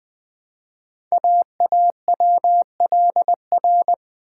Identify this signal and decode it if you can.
Morse code, heard as a keyed tone, about 20 wpm, 700 Hz. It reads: AAWLR
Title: Code Groups - 5 Letters - 20wpm